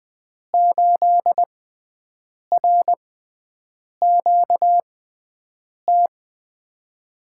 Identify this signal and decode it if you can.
Morse code, heard as a keyed tone, about 20 wpm, spaced as for 7 wpm, 700 Hz. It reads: 8RQT
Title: USA Call Signs - 5 Characters - 6x Character Spacing - 20wpm